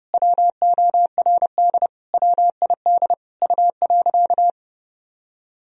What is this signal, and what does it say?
Morse code, heard as a keyed tone, about 30 wpm, 700 Hz. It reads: WORD WID U.
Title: Sentences from Top 300 Words and Abbreviations 30wpm